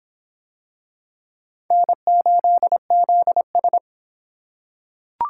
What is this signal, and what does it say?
Morse code, heard as a keyed tone, about 23 wpm, 700 Hz. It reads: N8ZS